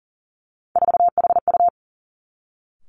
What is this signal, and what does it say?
Morse code, heard as a keyed tone, about 41 wpm, 700 Hz. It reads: 4HU
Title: Three Challenging Letter-Number Combinations 40wpm